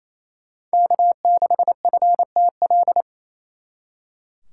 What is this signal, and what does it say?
Morse code, heard as a keyed tone, about 35 wpm, 700 Hz. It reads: K6FTL